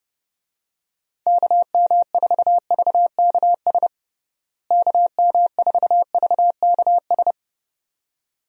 Morse code, heard as a keyed tone, about 32 wpm, 700 Hz. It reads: KM4VKS KM4VKS
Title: USA Call Signs - 6 Characters - Repeated 2x - 30wpm